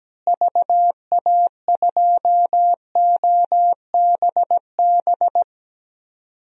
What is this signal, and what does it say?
Morse code, heard as a keyed tone, about 17 wpm, 700 Hz. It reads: VA2OBB
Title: DXCC-100 Call Signs - 6 Characters - Repeated 3x - 17wpm